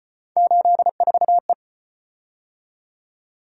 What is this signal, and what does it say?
Morse code, heard as a keyed tone, about 35 wpm, 700 Hz. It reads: 84E